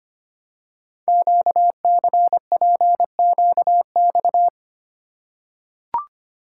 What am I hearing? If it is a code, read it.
Morse code, heard as a keyed tone, about 25 wpm, 700 Hz. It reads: QCPQX
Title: Code Groups - 5 Letters - 25wpm